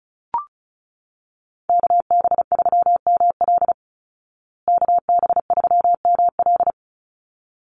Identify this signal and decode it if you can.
Morse code, heard as a keyed tone, about 34 wpm, 700 Hz. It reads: KB3ML KB3ML
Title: Call Signs Repeated 3x - 35wpm